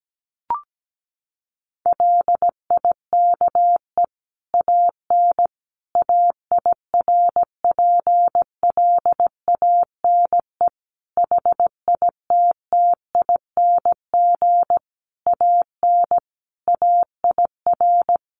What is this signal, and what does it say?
Morse code, heard as a keyed tone, about 17 wpm, 700 Hz. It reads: LIKE AN AIRPLANE HITTING AN AIR
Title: A Sound Of Thunder 17wpm